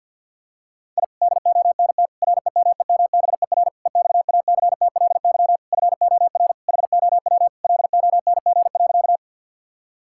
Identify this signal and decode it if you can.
Morse code, heard as a keyed tone, about 50 wpm, 700 Hz. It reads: I DONT REMEMBER EXACTLY FOR HOW LONG.
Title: Sentences from Top 1000 Words - Review Part 6 - 50wpm